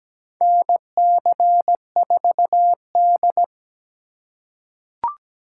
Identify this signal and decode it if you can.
Morse code, heard as a keyed tone, about 17 wpm, 700 Hz. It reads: NC4D
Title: USA Call Signs - 8x Character Spacing - 17wpm